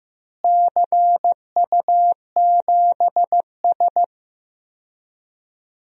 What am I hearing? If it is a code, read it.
Morse code, heard as a keyed tone, about 15 wpm, 700 Hz. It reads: CU7S
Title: DXCC-100 Call Signs - 4 Characters - Repeated 2x - 15wpm